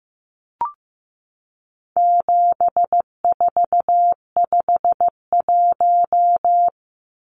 Morse code, heard as a keyed tone, about 15 wpm, 700 Hz. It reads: 7451
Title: Numbers - 4-Digit 15wpm